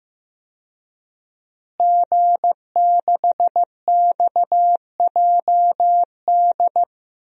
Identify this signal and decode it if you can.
Morse code, heard as a keyed tone, about 15 wpm, 700 Hz. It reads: G6XJD
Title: DXCC-100 Call Signs - 5 Characters - Repeated 2x - 15wpm